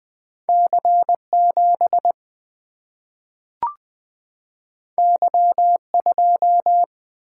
Morse code, heard as a keyed tone, about 20 wpm, 700 Hz. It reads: C 7 Y 2
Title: Single-Letter Single-Number 20wpm